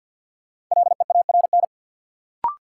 Morse code, heard as a keyed tone, about 47 wpm, 700 Hz. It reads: LEARN
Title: Sentences from Top 300 Words - New Word Drill - 50wpm